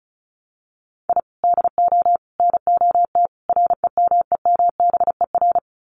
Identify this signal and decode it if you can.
Morse code, heard as a keyed tone, about 34 wpm, 700 Hz. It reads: I DO NOT REMEMBER
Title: Sets of 4 Words 35wpm